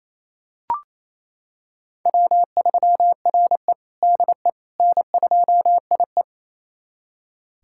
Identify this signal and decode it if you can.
Morse code, heard as a keyed tone, about 29 wpm, 700 Hz. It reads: W3RE DE N2IE
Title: USA Callsign DE Callsign - 4 Characters - 28wpm